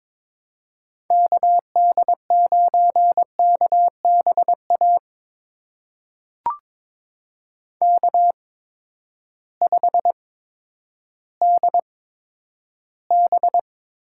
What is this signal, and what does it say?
Morse code, heard as a keyed tone, about 23 wpm, spaced as for 11 wpm, 700 Hz. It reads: KD9KBA K5DB
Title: USA Call Signs - 8x Character Spacing - 22wpm